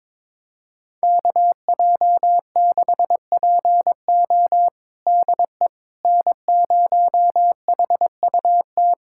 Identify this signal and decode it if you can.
Morse code, heard as a keyed tone, about 21 wpm, 700 Hz. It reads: KJ6PO DE N0HUT
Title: USA Callsign DE Callsign - 5 Characters - 22wpm